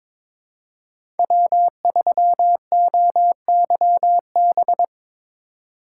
Morse code, heard as a keyed tone, about 22 wpm, 700 Hz. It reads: W3OYB
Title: USA Call Signs - 5 Characters - 3x Character Spacing - 22wpm